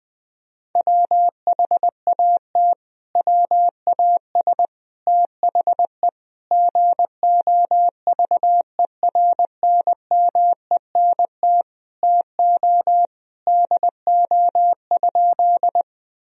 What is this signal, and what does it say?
Morse code, heard as a keyed tone, about 20 wpm, 700 Hz. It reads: WHAT WAS THE GOVERNMENT TO DO?
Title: Sentences from Top 1000 Words - Review Part 3 - 20wpm